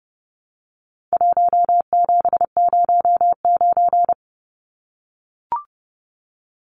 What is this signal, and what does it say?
Morse code, heard as a keyed tone, about 29 wpm, 700 Hz. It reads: 1709
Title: Numbers - Contest Style - 4-Digit 30wpm